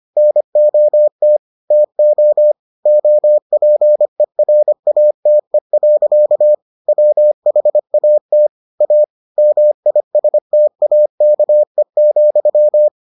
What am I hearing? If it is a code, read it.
Morse code, heard as a keyed tone, about 25 wpm, 600 Hz. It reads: NOT TO OPERATE. WHAT A MISTAKE,